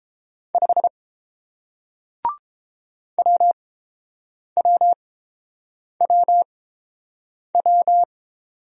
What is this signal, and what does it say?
Morse code, heard as a keyed tone, about 29 wpm, spaced as for 7 wpm, 700 Hz. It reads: 5 WWWW